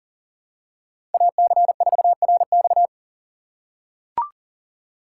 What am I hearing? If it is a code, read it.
Morse code, heard as a keyed tone, about 40 wpm, 700 Hz. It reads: AC4RX
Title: USA Call Signs - 5 Characters - 4x Character Spacing - 40wpm